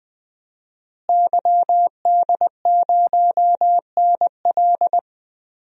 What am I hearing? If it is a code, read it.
Morse code, heard as a keyed tone, about 20 wpm, 700 Hz. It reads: YD0NL
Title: DXCC-100 Call Signs - 5 Characters - Repeated 2x - 20wpm